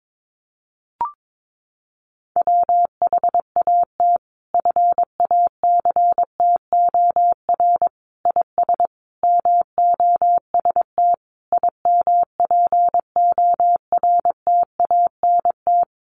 Morse code, heard as a keyed tone, about 22 wpm, 700 Hz. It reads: WHAT FACTOR IS MOST IMPORTANT
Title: Sentences from Top 1300 Words 22wpm